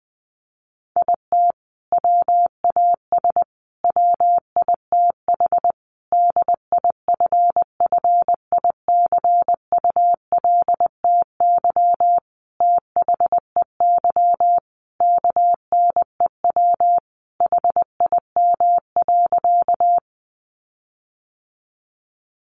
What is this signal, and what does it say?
Morse code, heard as a keyed tone, about 20 wpm, 700 Hz. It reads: IT WAS WITH DIFFICULTY THEY KNEW HIM.
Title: Sentences from Top 1500 Words 20wpm